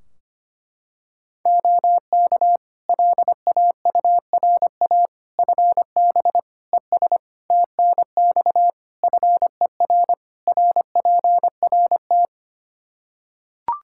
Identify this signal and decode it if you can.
Morse code, heard as a keyed tone, about 25 wpm, 700 Hz. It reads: OK LAURA FB ES TNX FER RPRT
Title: K1USN SST QSO Practice 25wpm